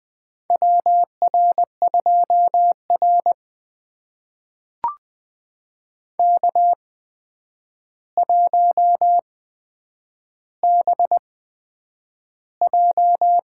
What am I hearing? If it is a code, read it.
Morse code, heard as a keyed tone, about 20 wpm, spaced as for 9 wpm, 700 Hz. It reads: WR2R K1BJ
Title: USA Call Signs - 4 Characters - 8x Character Spacing - 20wpm